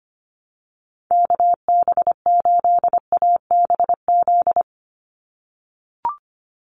Morse code, heard as a keyed tone, about 25 wpm, 700 Hz. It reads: KB8ABZ
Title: USA Call Signs 25wpm